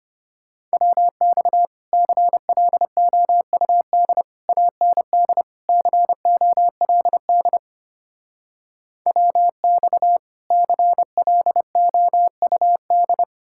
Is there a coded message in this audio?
Yes: Morse code from 0.7 to 13.2 s, heard as a keyed tone, about 33 wpm, 700 Hz. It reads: WX CLOUD AND COLD WX CLOUD